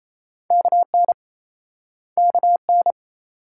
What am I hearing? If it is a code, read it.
Morse code, heard as a keyed tone, about 38 wpm, 700 Hz. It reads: KN KN